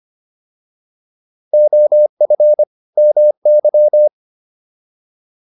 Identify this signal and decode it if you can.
Morse code, heard as a keyed tone, about 25 wpm, 600 Hz. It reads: OF MY